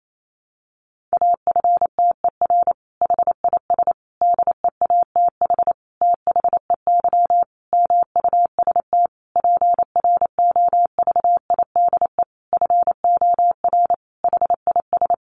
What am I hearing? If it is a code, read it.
Morse code, heard as a keyed tone, about 28 wpm, 700 Hz. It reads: AFTER HIS DEATH THEY MUST PROVIDE FOR HIS
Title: Sentences from Top 1100 Words 28wpm